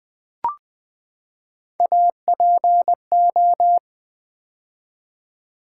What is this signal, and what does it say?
Morse code, heard as a keyed tone, about 20 wpm, 700 Hz. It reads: APO
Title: Build A Word - 3 Letters At A Time - 9-Letter Words - 20wpm